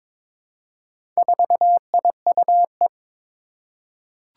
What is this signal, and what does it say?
Morse code, heard as a keyed tone, about 21 wpm, 700 Hz. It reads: 4IUE
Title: Four Challenging Letter-Number Combinations 22wpm